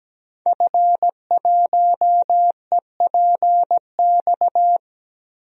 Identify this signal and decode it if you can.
Morse code, heard as a keyed tone, about 17 wpm, 700 Hz. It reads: F1EPX
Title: DXCC-100 Call Signs - 5 Characters - Repeated 2x - 17wpm